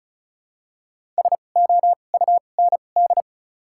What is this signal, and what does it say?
Morse code, heard as a keyed tone, about 34 wpm, spaced as for 25 wpm, 700 Hz. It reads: SOUND
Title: 2x Character Spacing - Top 500 Words - 35wpm